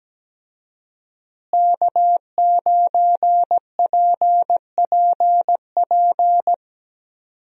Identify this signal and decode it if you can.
Morse code, heard as a keyed tone, about 17 wpm, 700 Hz. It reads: K9PPP
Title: USA Call Signs - 5 Characters - Repeated 3x - 17wpm